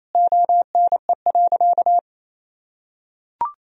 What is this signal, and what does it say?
Morse code, heard as a keyed tone, about 28 wpm, 700 Hz. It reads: ONE.
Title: Sentences from Top 400 Words and Abbreviations 28wpm